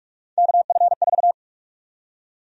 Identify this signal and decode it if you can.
Morse code, heard as a keyed tone, about 46 wpm, 700 Hz. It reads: KF4
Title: USA Call Sign Prefixes - Repeated 2x - 45wpm